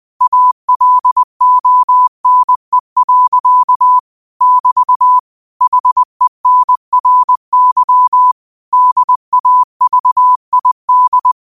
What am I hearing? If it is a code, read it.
Morse code, heard as a keyed tone, about 20 wpm, 1000 Hz. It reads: ALONE. <BT> HENRY DAVID
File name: qotdfiles / 20wpm